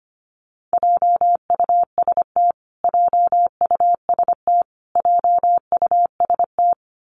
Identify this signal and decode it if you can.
Morse code, heard as a keyed tone, about 25 wpm, 700 Hz. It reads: JUST JUST JUST
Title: Top 200 Words Repeated 25wpm